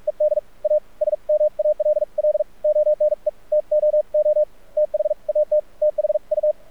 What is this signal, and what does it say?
Morse code, heard as a keyed tone, about 46 wpm, spaced as for 39 wpm, 600 Hz, with radio noise. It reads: ED A SMALL ONE TOO THAT THU
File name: archived / CRSnet